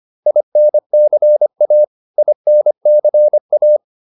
Morse code, heard as a keyed tone, about 25 wpm, 600 Hz. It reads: INCA INCA